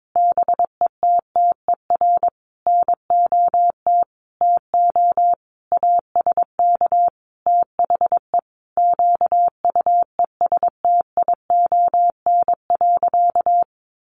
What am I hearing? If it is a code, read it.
Morse code, heard as a keyed tone, about 22 wpm, 700 Hz. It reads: BETTER NOT TO ASK THE QUESTION.
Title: Sentences from Top 600 Words 22wpm